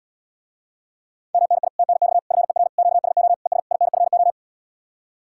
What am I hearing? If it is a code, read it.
Morse code, heard as a keyed tone, about 78 wpm, 700 Hz. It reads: MNI TNX FER QSO ES INFO